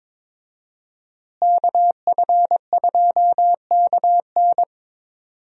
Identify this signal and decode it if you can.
Morse code, heard as a keyed tone, about 22 wpm, 700 Hz. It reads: KF2KN
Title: USA Call Signs - 5 Characters - Repeated 2x - 22wpm